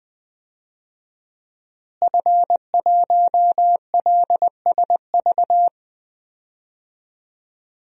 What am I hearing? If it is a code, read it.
Morse code, heard as a keyed tone, about 21 wpm, 700 Hz. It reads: F1LSV
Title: DXCC-100 Call Signs - 5 Characters - Repeated 2x - 20wpm